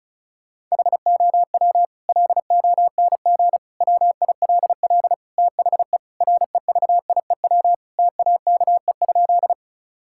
Morse code, heard as a keyed tone, about 36 wpm, 700 Hz. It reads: HOW LONG WILL THE REVIEW TAKE?
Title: Sentences from Top 1000 Words - Review Part 6 - 35wpm